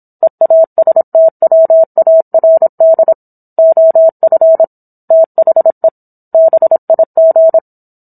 Morse code, heard as a keyed tone, about 27 wpm, 650 Hz. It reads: EASTWARD OF THE BIG